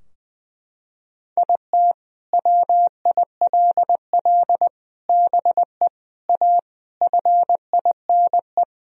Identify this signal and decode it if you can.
Morse code, heard as a keyed tone, about 19 wpm, 700 Hz. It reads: IT WILL BE A FINE
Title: Sentences from Top 1000 Words - Review Part 2 - 20wpm